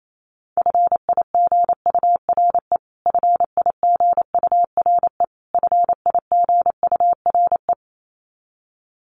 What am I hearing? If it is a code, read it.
Morse code, heard as a keyed tone, about 28 wpm, 700 Hz. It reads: FIGURE FIGURE FIGURE
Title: Top 500 Words Repeated 28wpm